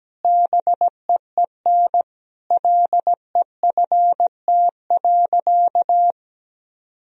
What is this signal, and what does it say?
Morse code, heard as a keyed tone, about 17 wpm, 700 Hz. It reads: BEEN LEFT.
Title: Sentences from Top 1000 Words - Review Part 2 - 17wpm